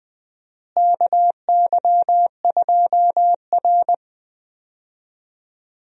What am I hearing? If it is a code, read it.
Morse code, heard as a keyed tone, about 20 wpm, 700 Hz. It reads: KY2R
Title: USA Call Signs - 4 Characters - Repeated 3x - 20wpm